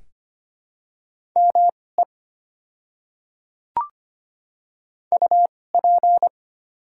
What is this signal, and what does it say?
Morse code, heard as a keyed tone, about 25 wpm, spaced as for 18 wpm, 700 Hz. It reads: ME UP
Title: 2x Character Spacing - Top 200 Words - 25wpm